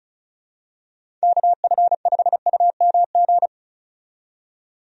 Morse code, heard as a keyed tone, about 34 wpm, 700 Hz. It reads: KF5UMG
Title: USA Call Signs - 6 Characters - 35wpm